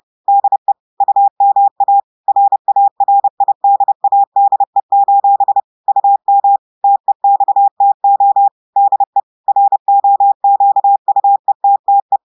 Morse code, heard as a keyed tone, about 30 wpm, 800 Hz. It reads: DE UMA RARIDADE: UM TEXTO DE ROQUETTE
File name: audios / portugais / portugais texte en clair 1 vitesse 30 mots minutes